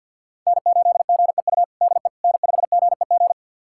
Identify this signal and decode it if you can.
Morse code, heard as a keyed tone, about 50 wpm, 690 Hz. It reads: N8GEU DE N5GEG
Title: USA Callsign DE Callsign - 5 Characters - 50wpm